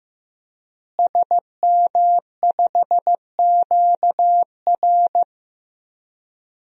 Morse code, heard as a keyed tone, about 15 wpm, 700 Hz. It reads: SM5QR
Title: DXCC-100 Call Signs - 5 Characters - Repeated 3x - 15wpm